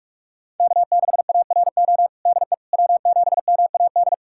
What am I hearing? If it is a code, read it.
Morse code, heard as a keyed tone, about 47 wpm, 700 Hz. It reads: K6ARO DE W7MAD
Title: USA Callsign DE Callsign - 5 Characters - 45wpm